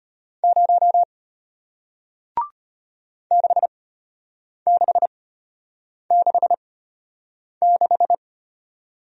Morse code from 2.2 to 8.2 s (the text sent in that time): 6666